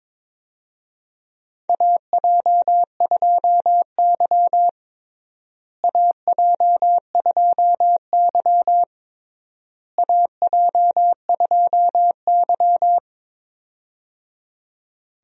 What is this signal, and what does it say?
Morse code, heard as a keyed tone, about 22 wpm, 700 Hz. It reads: AJ2Y AJ2Y AJ2Y
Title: USA Call Signs - 4 Characters - Repeated 3x - 22wpm